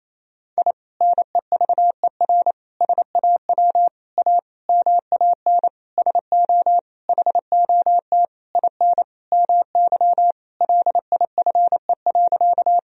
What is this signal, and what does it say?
Morse code, heard as a keyed tone, about 27 wpm, 700 Hz. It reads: I NEVER SAW A MAN SO HOT IN MY LIFE.
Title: Sentences from Top 1000 Words - Review Part 3 - 28wpm